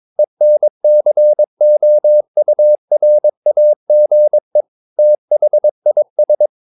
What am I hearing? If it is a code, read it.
Morse code, heard as a keyed tone, about 22 wpm, 600 Hz. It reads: ENCOURAGE THIS
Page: Romanen Never af Ken Follet på engelsk.